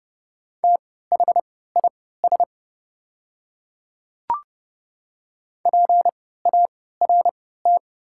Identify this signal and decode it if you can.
Morse code, heard as a keyed tone, about 30 wpm, spaced as for 17 wpm, 700 Hz. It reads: THIS PART
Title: 3x Character Spacing - Top 200 Words - 30wpm